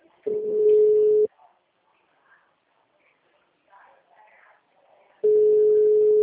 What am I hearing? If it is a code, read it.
Ringback tone (Europe, 425 Hz).